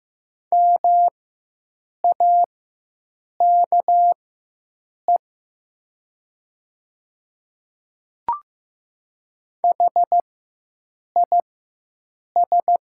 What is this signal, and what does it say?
Morse code, heard as a keyed tone, about 15 wpm, spaced as for 7 wpm, 700 Hz. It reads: MAKE HIS
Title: 4x Character Spacing - Top 200 Words - 15wpm